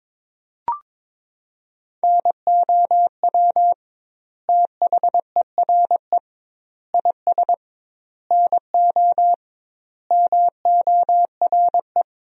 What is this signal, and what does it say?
Morse code, heard as a keyed tone, about 22 wpm, 700 Hz. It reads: NOW THERE IS NO MORE
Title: 2x Word-Spacing - Sentences from Top 100 Words - 22wpm